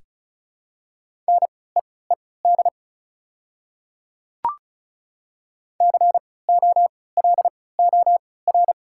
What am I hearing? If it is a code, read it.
Morse code, heard as a keyed tone, about 36 wpm, spaced as for 20 wpm, 700 Hz. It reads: NEED COLOR
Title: 3x Character Spacing - Top 300 Words - 35wpm